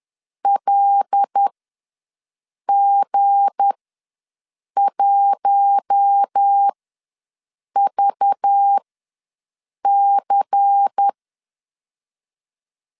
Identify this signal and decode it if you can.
Morse code, heard as a keyed tone, about 10 wpm, spaced as for 6 wpm, 800 Hz. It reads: LG1VC
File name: LEZIONI / 7 GF9Z / 7 GF ws 10wpm 3918